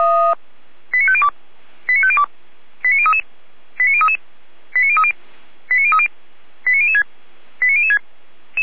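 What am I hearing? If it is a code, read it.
Five-tone selective call (ZVEI), heard as quick runs of five tones, tones of about 70 ms. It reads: 89502 89502 89033 89033 89030 89030 89006 89006
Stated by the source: ZVEI five-tone selective call received by SDR.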